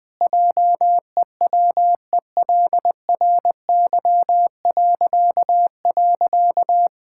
Morse code, heard as a keyed tone, about 21 wpm, 700 Hz. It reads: JEWELRY..
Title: Pangram Challenge v2 - 20wpm